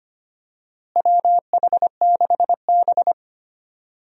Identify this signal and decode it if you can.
Morse code, heard as a keyed tone, about 25 wpm, 700 Hz. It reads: WH6B
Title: USA Call Signs - 4 Characters - Repeated 2x - 25wpm